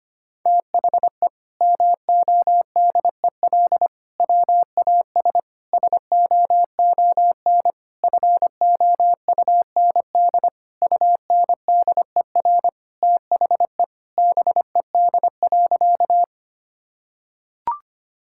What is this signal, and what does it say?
Morse code, heard as a keyed tone, about 25 wpm, 700 Hz. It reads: THE MODEL WAS SOON FOUND UNDER THE BED.
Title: Sentences from Top 1000 Words - Review Part 5 - 25wpm